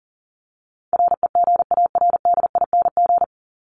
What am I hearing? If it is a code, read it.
Morse code, heard as a keyed tone, about 40 wpm, 700 Hz. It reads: REGARDING
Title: ING Suffix - Top 100 Words with Base Words 40wpm